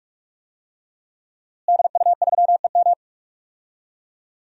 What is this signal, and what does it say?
Morse code, heard as a keyed tone, about 46 wpm, 700 Hz. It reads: DU3EM
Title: DXCC-100 Call Signs - 5 Characters - Repeated 3x - 45wpm